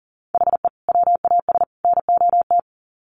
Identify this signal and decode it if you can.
Morse code, heard as a keyed tone, about 40 wpm, 700 Hz. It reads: HE WAS NOT
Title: Sets of 3 Words 40wpm